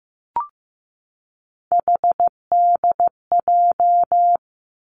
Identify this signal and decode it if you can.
Morse code, heard as a keyed tone, about 15 wpm, 700 Hz. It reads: HDJ
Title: Three Challenging Letter-Number Combinations 15wpm